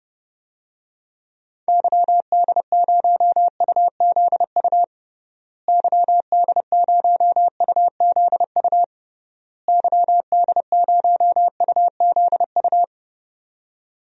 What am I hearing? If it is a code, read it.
Morse code, heard as a keyed tone, about 32 wpm, 700 Hz. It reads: YD0UZU YD0UZU YD0UZU
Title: DXCC-100 Call Signs - 6 Characters - Repeated 3x - 30wpm